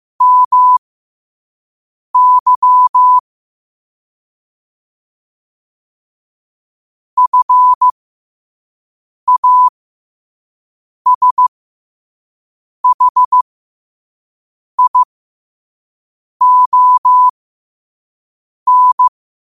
Morse code, heard as a keyed tone, about 15 wpm, spaced as for 5 wpm, 1000 Hz. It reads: MY FASHION